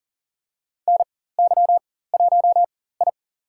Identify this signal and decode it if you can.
Morse code, heard as a keyed tone, about 42 wpm, spaced as for 19 wpm, 700 Hz. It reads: NY1I
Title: USA Call Signs - 4 Characters - 4x Character Spacing - 40wpm